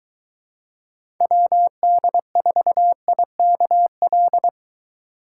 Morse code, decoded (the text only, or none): WD4IKL